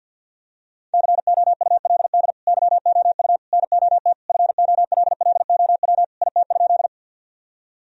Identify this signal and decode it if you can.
Morse code, heard as a keyed tone, about 52 wpm, 700 Hz. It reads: COULD YOU NOT FOLLOW IT?
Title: Sentences from Top 1000 Words - Review Part 1 - 50wpm